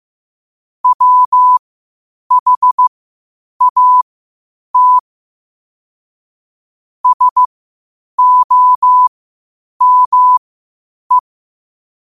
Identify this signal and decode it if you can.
Morse code, heard as a keyed tone, about 15 wpm, spaced as for 8 wpm, 1000 Hz. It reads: WHAT SOME
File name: qotdfiles / 8wpm